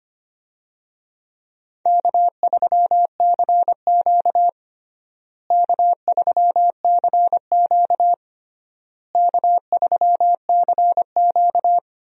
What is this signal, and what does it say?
Morse code, heard as a keyed tone, about 25 wpm, 700 Hz. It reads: K3CQ K3CQ K3CQ
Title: USA Call Signs - 4 Characters - Repeated 3x - 25wpm